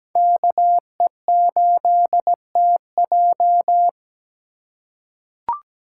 Morse code, heard as a keyed tone, about 17 wpm, 700 Hz. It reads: KE8TJ
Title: USA Call Signs - 5 Characters - 3x Character Spacing - 17wpm